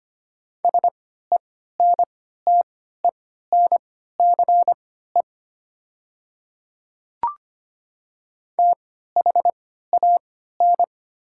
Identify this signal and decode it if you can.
Morse code, heard as a keyed tone, about 25 wpm, spaced as for 14 wpm, 700 Hz. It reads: SENTENCE THAN
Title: 3x Character Spacing - Top 300 Words - 25wpm